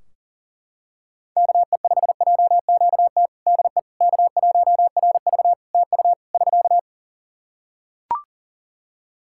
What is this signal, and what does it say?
Morse code, heard as a keyed tone, about 40 wpm, 700 Hz. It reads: KE5JQT DE K1RV TU <SK>
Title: Practice QSO - Exchange 4 - Ending - 40wpm